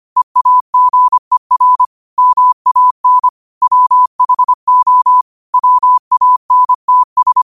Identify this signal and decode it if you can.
Morse code, heard as a keyed tone, about 25 wpm, 1000 Hz. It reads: EAGER MAN WHO WANTS
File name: qotdfiles / 25wpm